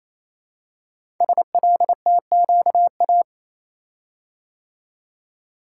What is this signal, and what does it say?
Morse code, heard as a keyed tone, about 27 wpm, 700 Hz. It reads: SLTQA
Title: Code Groups - 5 Letters - 28wpm